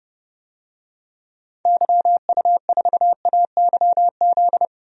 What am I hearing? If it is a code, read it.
Morse code, heard as a keyed tone, about 30 wpm, 700 Hz. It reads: YU4AYZ
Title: DXCC-100 Call Signs - 6 Characters - Repeated 3x - 30wpm